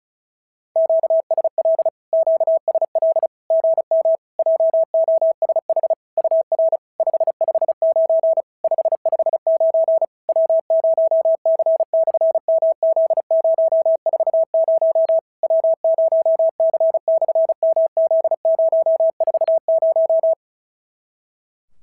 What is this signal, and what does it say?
Morse code, heard as a keyed tone, about 36 wpm, 650 Hz. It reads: QSL QSL GM JOSH UR 559 559 W0C/MZ040 W0C/MZ040
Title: SOTA QSO Practice 35wpm